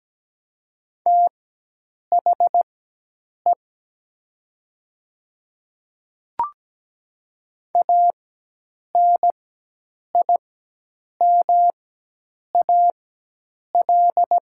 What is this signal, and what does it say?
Morse code, heard as a keyed tone, about 17 wpm, spaced as for 8 wpm, 700 Hz. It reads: THE ANIMAL
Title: 4x Character Spacing - Top 300 Words - 17wpm